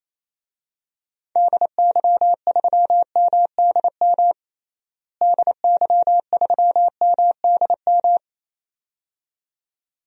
Morse code, heard as a keyed tone, about 28 wpm, 700 Hz. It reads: DY3MDM DY3MDM
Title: DXCC-100 Call Signs - 6 Characters - Repeated 2x - 28wpm